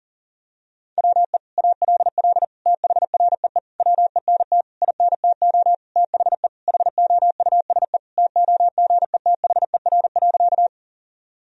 Morse code, heard as a keyed tone, about 40 wpm, 700 Hz. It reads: WE ALL THREE WENT INTO THE HOUSE TOGETHER.
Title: Sentences from Top 1000 Words - Review Part 1 - 40wpm